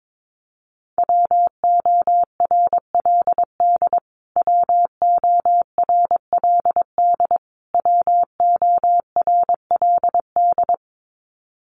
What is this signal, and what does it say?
Morse code, heard as a keyed tone, about 22 wpm, 700 Hz. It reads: WORLD WORLD WORLD
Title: Top 500 Words Repeated 22wpm